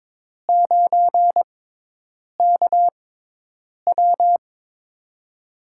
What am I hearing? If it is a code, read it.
Morse code, heard as a keyed tone, about 22 wpm, spaced as for 8 wpm, 700 Hz. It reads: 9KW
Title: USA Call Signs - 5 Characters - 6x Character Spacing - 22wpm